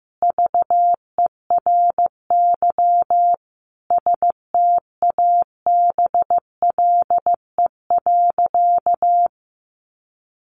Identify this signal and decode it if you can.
Morse code, heard as a keyed tone, about 15 wpm, 700 Hz. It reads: VERY STABLE.
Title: Sentences from Top 1500 Words 15wpm